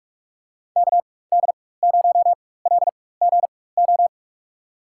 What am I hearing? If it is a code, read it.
Morse code, heard as a keyed tone, about 46 wpm, spaced as for 21 wpm, 700 Hz. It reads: KD0LGO
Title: USA Call Signs - 6 Characters - 4x Character Spacing - 45wpm